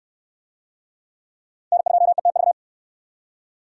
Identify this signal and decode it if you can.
Morse code, heard as a keyed tone, about 67 wpm, 700 Hz. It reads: N2ETV